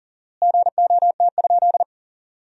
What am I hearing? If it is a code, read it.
Morse code, heard as a keyed tone, about 40 wpm, 700 Hz. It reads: GOT?
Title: 4x Word-Spacing - Sentences from Top 300 Words - 40wpm